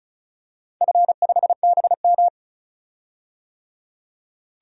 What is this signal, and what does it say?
Morse code, heard as a keyed tone, about 36 wpm, 700 Hz. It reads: F5BM
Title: DXCC-100 Call Signs - 4 Characters - 35wpm